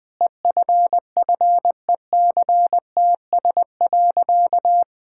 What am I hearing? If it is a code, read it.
Morse code, heard as a keyed tone, about 19 wpm, 700 Hz. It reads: EFFECTS.
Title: Sentences with 5-Word Phrases - 20wpm